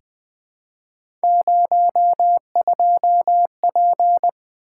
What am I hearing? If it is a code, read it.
Morse code, heard as a keyed tone, about 20 wpm, 700 Hz. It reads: 02P
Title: Three Letter-Number Combinations - 20wpm